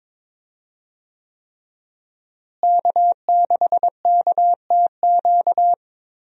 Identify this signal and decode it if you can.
Morse code, heard as a keyed tone, about 22 wpm, 700 Hz. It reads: K6KTQ
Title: USA Call Signs - 5 Characters - Repeated 3x - 22wpm